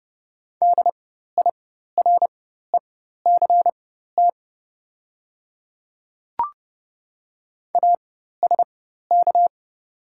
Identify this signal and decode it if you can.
Morse code, heard as a keyed tone, about 30 wpm, spaced as for 14 wpm, 690 Hz. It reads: DIRECT ASK
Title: 4x Character Spacing - Top 500 Words - 30wpm